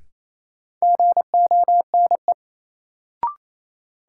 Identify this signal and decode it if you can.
Morse code, heard as a keyed tone, about 27 wpm, 700 Hz. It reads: GONE
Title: Sentences from Top 1200 Words - New Word Drill Alt - 28wpm